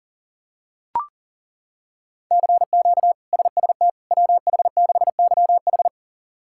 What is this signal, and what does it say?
Morse code, heard as a keyed tone, about 37 wpm, 700 Hz. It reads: CQ SST WH6YH
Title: K1USN SST QSO Practice - 40wpm